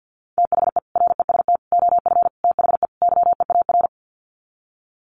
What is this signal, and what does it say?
Morse code, heard as a keyed tone, about 50 wpm, 700 Hz. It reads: THE REST OF THE YEAR
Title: Sets of 5 Words Encore 50wpm